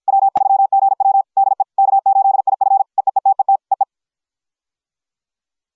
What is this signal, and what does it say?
Morse code, heard as a keyed tone, about 53 wpm, 750 Hz. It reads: W3GW DE K8IF EEETET EE